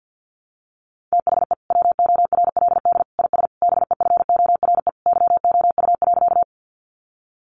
Morse code, heard as a keyed tone, about 50 wpm, 700 Hz. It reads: THE WORLD IS BEFORE YOU.
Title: Sentences from Top 200 Words 50wpm